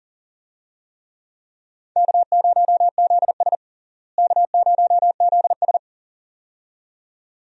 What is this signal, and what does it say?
Morse code, heard as a keyed tone, about 40 wpm, 700 Hz. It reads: K0ZS K0ZS
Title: USA Call Signs - 4 Characters - Repeated 2x - 40wpm